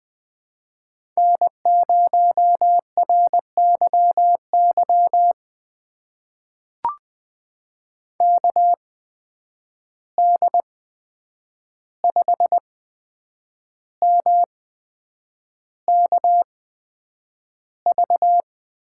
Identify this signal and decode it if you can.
Morse code, heard as a keyed tone, about 20 wpm, spaced as for 8 wpm, 700 Hz. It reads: N0RYY KD5MKV
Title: USA Call Signs - 8x Character Spacing - 20wpm